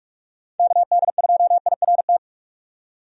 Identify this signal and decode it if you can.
Morse code, heard as a keyed tone, about 46 wpm, 700 Hz. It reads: KD2IRT
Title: USA Call Signs - 6 Characters - 45wpm